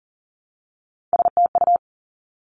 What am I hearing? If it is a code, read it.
Morse code, heard as a keyed tone, about 39 wpm, 690 Hz. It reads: STU
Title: Three Challenging Letter-Number Combinations 40wpm